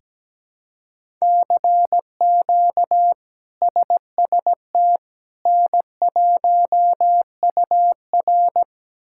Morse code, heard as a keyed tone, about 17 wpm, 700 Hz. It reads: CQ SST N1UR
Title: K1USN SST QSO Practice - 17wpm